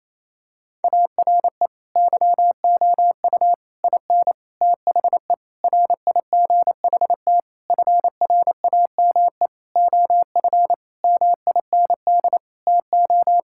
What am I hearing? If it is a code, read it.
Morse code, heard as a keyed tone, about 28 wpm, 700 Hz. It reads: ARE YOU IN THE RIGHT FRAME OF MIND TO
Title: Sentences from Top 1000 Words - Review Part 6 - 28wpm